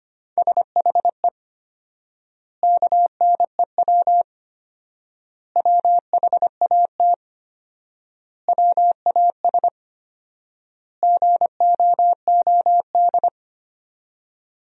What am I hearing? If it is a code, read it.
Morse code, heard as a keyed tone, about 25 wpm, 700 Hz. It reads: SHE KNEW WHAT WAS GOOD
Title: 4x Word-Spacing - Sentences from Top 400 Words - 25wpm